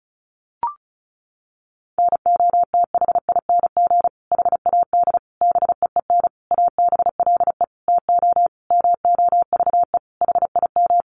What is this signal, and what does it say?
Morse code, heard as a keyed tone, about 34 wpm, 700 Hz. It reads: NOTHING HAD BEEN ABLE TO MOVE HIM
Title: Sentences from Top 1300 Words 35wpm